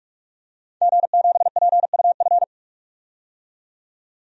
Morse code, heard as a keyed tone, about 46 wpm, 700 Hz. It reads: G7PUF
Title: DXCC-100 Call Signs - 5 Characters - 45wpm